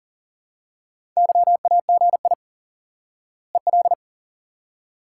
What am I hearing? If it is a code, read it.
Morse code, heard as a keyed tone, about 37 wpm, 700 Hz. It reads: YAGI EL
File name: IQE Course 3 - Lesson 89 - YAGI - Two Elements - 40wpm